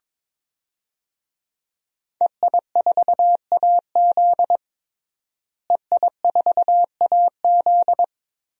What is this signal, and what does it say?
Morse code, heard as a keyed tone, about 22 wpm, 700 Hz. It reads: EI4AZ EI4AZ
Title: DXCC-100 Call Signs - 5 Characters - Repeated 2x - 22wpm